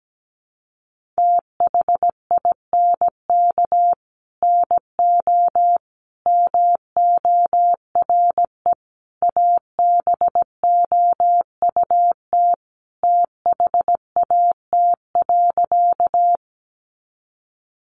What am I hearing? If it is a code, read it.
Morse code, heard as a keyed tone, about 17 wpm, 700 Hz. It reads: THINK NO MORE ABOUT THAT.
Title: Sentences from Top 200 Words 17wpm